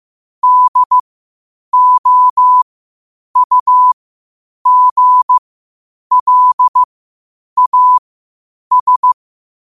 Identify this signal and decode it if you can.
Morse code, heard as a keyed tone, about 15 wpm, spaced as for 8 wpm, 1000 Hz. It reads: DOUGLAS
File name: qotdfiles / 8wpm